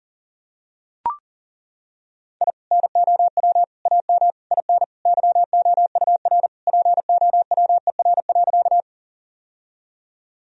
Morse code, heard as a keyed tone, about 37 wpm, 700 Hz. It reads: I NOW AM IN YOUR POWER.
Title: Sentences from Top 1000 Words - Review Part 2 - 40wpm